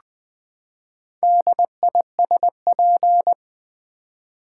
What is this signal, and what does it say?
Morse code, heard as a keyed tone, about 20 wpm, 700 Hz. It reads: DISP
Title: Build A Word - 4 Letters At A Time - 8-Letter Words - 20wpm